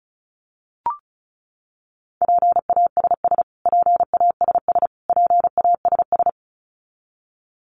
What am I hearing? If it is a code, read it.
Morse code, heard as a keyed tone, about 34 wpm, 700 Hz. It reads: PASS PASS PASS
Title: Top 500 Words Repeated 35wpm